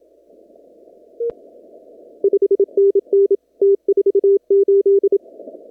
Morse code, heard as a keyed tone, about 27 wpm, 400 Hz. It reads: 5NN T48
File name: CW / kt5j wpx cw-op-k5pi